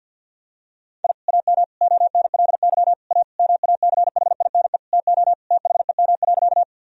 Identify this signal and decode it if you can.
Morse code, heard as a keyed tone, about 46 wpm, 700 Hz. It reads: I AM ONLY A MACHINE TO THEM.
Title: Sentences from Top 1000 Words - Review Part 2 - 50wpm